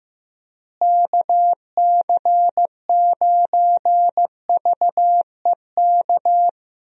Morse code, heard as a keyed tone, about 15 wpm, 700 Hz. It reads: KC9VEK
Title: USA Call Signs - 6 Characters - Repeated 3x - 15wpm